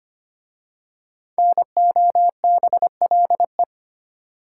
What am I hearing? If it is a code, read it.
Morse code, heard as a keyed tone, about 25 wpm, 700 Hz. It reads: NOBLE
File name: 5 Letter Words - First 3 Letters Given - 25wpm